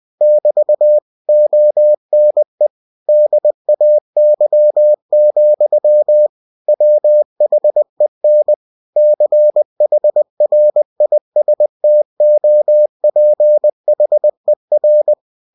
Morse code, heard as a keyed tone, about 20 wpm, 600 Hz. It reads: <BT> ONE DAY, WHEN CHRISTOPHER
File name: cw-books / winnie-the-pooh / winnie-20wpm-page-0033